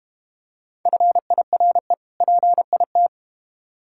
Morse code, heard as a keyed tone, about 32 wpm, 700 Hz. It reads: FIRE PIT